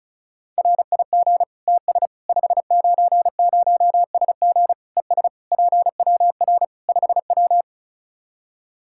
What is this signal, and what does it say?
Morse code, heard as a keyed tone, about 36 wpm, 700 Hz. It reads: RIG TS 590SG ES PWR 5W
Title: Practice QSO - Exchange 2 - Info - 35wpm